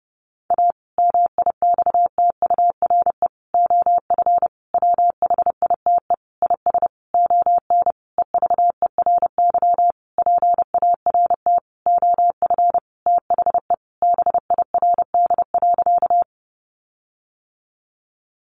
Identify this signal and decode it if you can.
Morse code, heard as a keyed tone, about 30 wpm, 700 Hz. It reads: A MIXTURE OF WHITE IS ON EVERY PART OF THE BIRD.
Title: Sentences from Top 1500 Words 30wpm